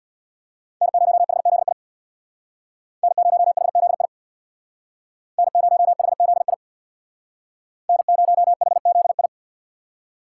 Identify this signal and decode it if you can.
Morse code, heard as a keyed tone, about 62 wpm, 700 Hz. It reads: N0HZI N0HZI N0HZI N0HZI
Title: Speed Racing - USA 1x3 Call Signs 50wpm